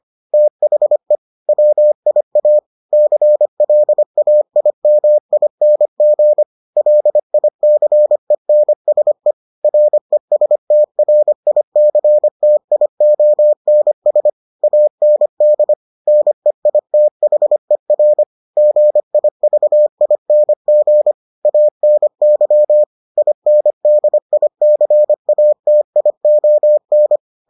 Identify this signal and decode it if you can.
Morse code, heard as a keyed tone, about 25 wpm, 600 Hz. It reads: THE WIA CLAIMING LICENSE RESTRICTIONS AND NEITHER GIVING ANY INDICATION